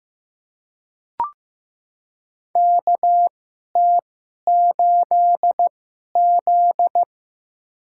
Morse code, heard as a keyed tone, about 15 wpm, spaced as for 11 wpm, 700 Hz. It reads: KT8Z
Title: USA Call Signs - 4 Characters - 2x Character Spacing - 15wpm